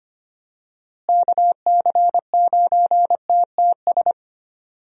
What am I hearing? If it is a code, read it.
Morse code, heard as a keyed tone, about 25 wpm, 700 Hz. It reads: KC9TTS
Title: USA Call Signs - 6 Characters - Repeated 3x - 25wpm